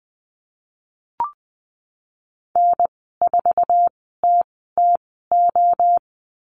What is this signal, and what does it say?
Morse code, heard as a keyed tone, about 20 wpm, spaced as for 15 wpm, 700 Hz. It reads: N4TTO
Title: 048 Lesson 9g - Number 4 - Call Signs 20wpm